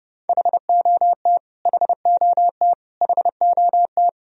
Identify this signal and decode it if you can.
Morse code, heard as a keyed tone, about 30 wpm, 700 Hz. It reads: HOT HOT HOT
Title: Top 100 Words Repeated 30wpm